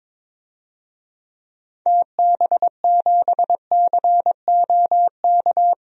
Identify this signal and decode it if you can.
Morse code, heard as a keyed tone, about 22 wpm, 700 Hz. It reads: TB7COK
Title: DXCC-100 Call Signs - 6 Characters - Repeated 3x - 22wpm